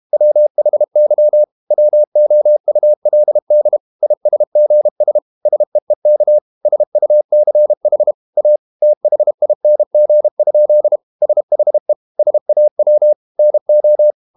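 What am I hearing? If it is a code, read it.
Morse code, heard as a keyed tone, about 32 wpm, 600 Hz. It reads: WHY WOULD ISGS SEEK SUCH A THING? SHE SAW NO
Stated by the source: Morse code practice at 32 wpm from CW med Gnister, reading Ken Follett's novel Never in English.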